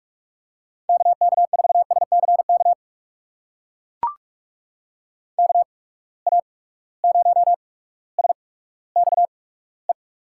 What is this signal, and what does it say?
Morse code, heard as a keyed tone, about 44 wpm, spaced as for 19 wpm, 700 Hz. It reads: KK4SCK KA0SXE
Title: USA Call Signs - 6 Characters - 8x Character Spacing - 45wpm